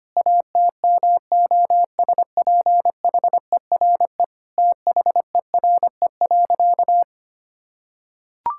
Morse code, heard as a keyed tone, about 25 wpm, 700 Hz. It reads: ATMOSPHERE THERE.
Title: Sentences from Top 1600 Words 25wpm